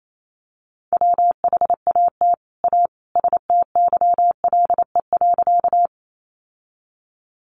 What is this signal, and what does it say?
Morse code, heard as a keyed tone, about 28 wpm, 700 Hz. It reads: WHAT A STYLE.
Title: Sentences from Top 800 Words 28wpm